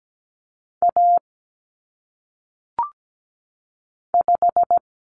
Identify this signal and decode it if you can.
Morse code, heard as a keyed tone, about 17 wpm, 700 Hz. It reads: A 5
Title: Common Cut Numbers - 1-Digit 17wpm